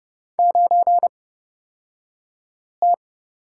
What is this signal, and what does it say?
Morse code, heard as a keyed tone, about 30 wpm, spaced as for 5 wpm, 700 Hz. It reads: 9T
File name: Lesson 38d - Letter q - Two Characters 30wpm